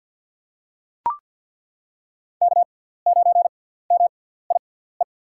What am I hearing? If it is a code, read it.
Morse code, heard as a keyed tone, about 48 wpm, spaced as for 17 wpm, 700 Hz. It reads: K9MIE
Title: USA Call Signs - 5 Characters - 6x Character Spacing - 50wpm